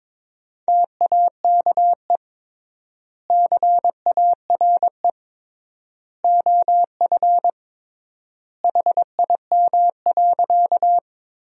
Morse code, heard as a keyed tone, about 22 wpm, 700 Hz. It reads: TAKE CARE OF HIM.
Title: 3x Word-Spacing - Sentences from Top 300 Words - 22wpm